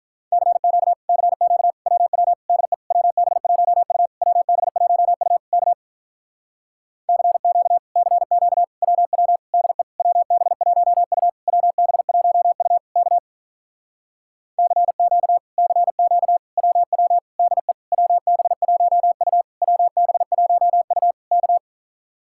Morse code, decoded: CQ CQ WW DE WB1U WB1U K CQ CQ WW DE WB1U WB1U K CQ CQ WW DE WB1U WB1U K

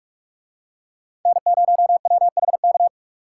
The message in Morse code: N0WHK

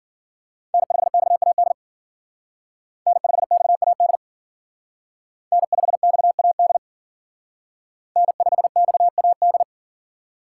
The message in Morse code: N5XAD N5XAD N5XAD N5XAD

